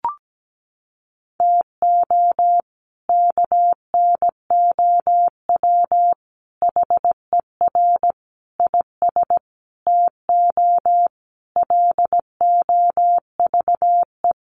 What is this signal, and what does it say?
Morse code, heard as a keyed tone, about 17 wpm, 700 Hz. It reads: TO KNOW HER IS TO LOVE
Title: Sentences from Top 400 Words 17wpm